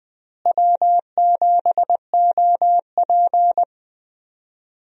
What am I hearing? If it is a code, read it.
Morse code, heard as a keyed tone, about 20 wpm, 700 Hz. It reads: W7OP
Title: USA Call Signs - 4 Characters - Repeated 3x - 20wpm